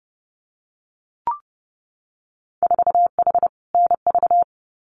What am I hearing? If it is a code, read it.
Morse code, heard as a keyed tone, about 30 wpm, 700 Hz. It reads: 4H NV
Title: Winter Field Day Exchanges Encore 30wpm